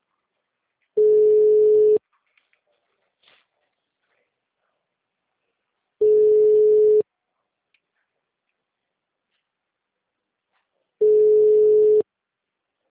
Ringback tone (Europe, 425 Hz).
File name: out- -2013- - - .61812